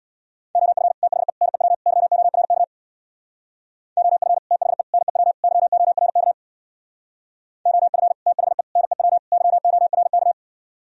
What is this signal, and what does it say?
Morse code, heard as a keyed tone, about 62 wpm, 700 Hz. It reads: OF THE NEW YORK OF THE NEW YORK OF THE NEW YORK
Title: Speed-Racing - Sets of 4 Words 50wpm